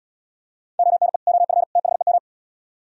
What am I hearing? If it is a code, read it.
Morse code, heard as a keyed tone, about 77 wpm, 700 Hz. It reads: ONE OF THEM